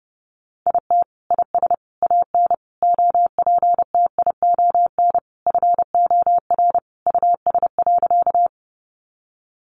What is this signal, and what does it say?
Morse code, heard as a keyed tone, about 30 wpm, 700 Hz. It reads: IT IS AN OPTION FOR US.
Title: Sentences from Top 1300 Words 30wpm